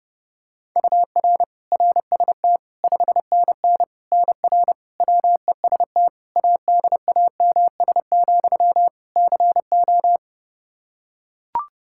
Morse code, heard as a keyed tone, about 32 wpm, 700 Hz. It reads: UR RST 5NN NR WEST ADAMS, CO
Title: Practice QSO - Exchange 1 - Beginning - 30wpm